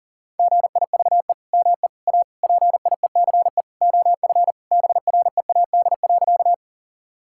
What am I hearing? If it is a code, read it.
Morse code, heard as a keyed tone, about 37 wpm, 700 Hz. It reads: GIVE ME A PIECE OF BREAD.
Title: Sentences from Top 1600 Words 40wpm